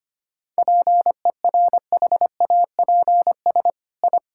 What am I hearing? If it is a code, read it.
Morse code, heard as a keyed tone, about 25 wpm, 700 Hz. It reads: PERHAPS I
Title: Sentences from Top 1000 Words - Review Part 5 - 25wpm